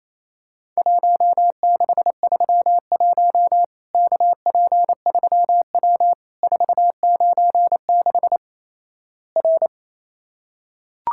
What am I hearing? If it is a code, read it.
Morse code, heard as a keyed tone, about 27 wpm, 700 Hz. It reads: 1631 KP3W 496 R
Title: Worked All Europe DX Contest - QTC Exchange - 28wpm